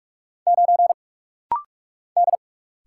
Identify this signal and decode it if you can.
Morse code, heard as a keyed tone, about 46 wpm, 700 Hz. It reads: 9 D
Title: Single Letter-Number Alt - Rapid-Fire 45wpm